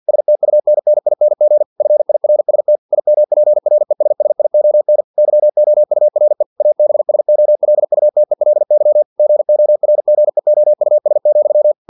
Morse code, heard as a keyed tone, about 50 wpm, 600 Hz. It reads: STUNNING FIRST IMPRESSION YOURE ABSOLUTELY GORGEOUS,
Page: Never 49wpm